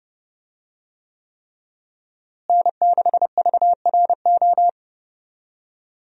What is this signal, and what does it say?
Morse code, heard as a keyed tone, about 30 wpm, 700 Hz. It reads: N6VRO
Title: USA Call Signs - 5 Characters - Repeated 3x - 30wpm